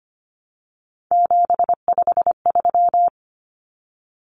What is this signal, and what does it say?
Morse code, heard as a keyed tone, about 25 wpm, 700 Hz. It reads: 753
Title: Common Cut Numbers - 3-Digit 25wpm